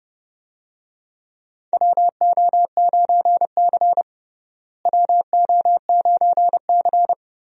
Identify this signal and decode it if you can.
Morse code, heard as a keyed tone, about 32 wpm, 700 Hz. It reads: WO9C WO9C
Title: USA Call Signs - 4 Characters - Repeated 3x - 30wpm